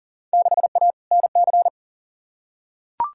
Morse code, heard as a keyed tone, about 40 wpm, 700 Hz. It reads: 6A NC
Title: Field Day Exchanges Encore - 40wpm